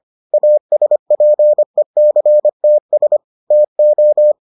Morse code, heard as a keyed tone, about 25 wpm, 600 Hz. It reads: ASPECTS TO